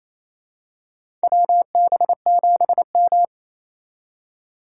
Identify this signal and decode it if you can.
Morse code, heard as a keyed tone, about 28 wpm, 700 Hz. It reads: WB7M
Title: USA Call Signs - 4 Characters - 3x Character Spacing - 28wpm